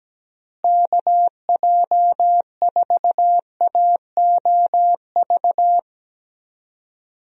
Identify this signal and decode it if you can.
Morse code, heard as a keyed tone, about 17 wpm, 700 Hz. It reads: KJ4AOV
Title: USA Call Signs - 6 Characters - Repeated 2x - 17wpm